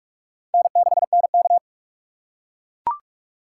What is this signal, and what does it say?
Morse code, heard as a keyed tone, about 42 wpm, 690 Hz. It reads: N6NK
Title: USA Call Signs - 4 Characters - 3x Character Spacing - 45wpm